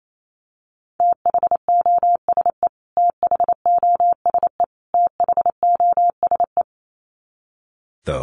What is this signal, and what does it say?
Morse code, heard as a keyed tone, about 28 wpm, 700 Hz. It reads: THOSE THOSE THOSE
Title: Top 500 Words Repeated 28wpm